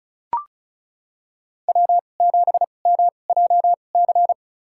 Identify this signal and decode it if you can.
Morse code, heard as a keyed tone, about 36 wpm, spaced as for 25 wpm, 700 Hz. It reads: W7MJC
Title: USA Call Signs - 5 Characters - 2x Character Spacing - 35wpm